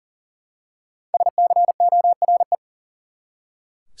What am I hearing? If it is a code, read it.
Morse code, heard as a keyed tone, about 40 wpm, 700 Hz. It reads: SCORE
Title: Five Letter Words - 40wpm